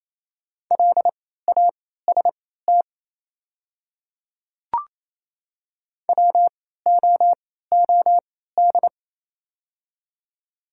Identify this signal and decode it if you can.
Morse code, heard as a keyed tone, about 27 wpm, spaced as for 16 wpm, 700 Hz. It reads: LAST WOOD
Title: 3x Character Spacing - Top 300 Words - 28wpm